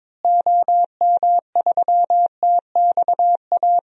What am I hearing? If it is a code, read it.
Morse code, heard as a keyed tone, about 22 wpm, 700 Hz. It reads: OM3TXA
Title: DXCC-100 Call Signs - 6 Characters - Repeated 2x - 22wpm